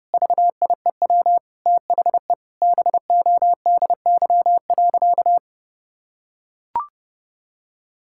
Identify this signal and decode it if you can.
Morse code, heard as a keyed tone, about 30 wpm, 700 Hz. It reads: VIEW THE BODY.
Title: Sentences from Top 1000 Words - Review Part 6 - 30wpm